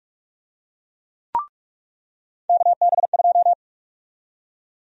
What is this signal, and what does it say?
Morse code, heard as a keyed tone, about 44 wpm, 700 Hz. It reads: KB2
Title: USA Call Sign Prefixes - 45wpm